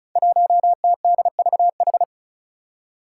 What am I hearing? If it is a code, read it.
Morse code, heard as a keyed tone, about 34 wpm, 700 Hz. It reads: 1TDVH
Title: Five Challenging Letter-Number Combinations 35wpm